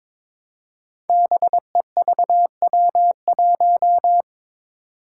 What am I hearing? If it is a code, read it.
Morse code, heard as a keyed tone, about 22 wpm, 700 Hz. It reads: BEVW1
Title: Five Challenging Letter-Number Combinations 22wpm